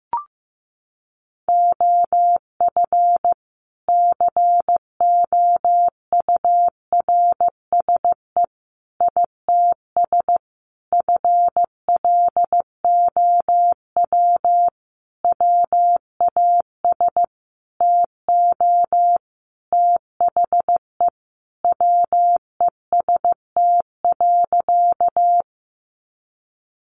Morse code, heard as a keyed tone, about 15 wpm, 700 Hz. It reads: OF COURSE ITS FLOW WAS TO THE WEST.
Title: Sentences from Top 1200 Words 15wpm